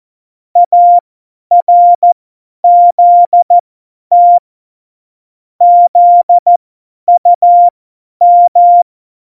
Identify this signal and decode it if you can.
Morse code, heard as a keyed tone, about 14 wpm, spaced as for 10 wpm, 700 Hz. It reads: ARZT ZUM
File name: Morsekurs / Witze / Witz 13 March 2018, 50 F70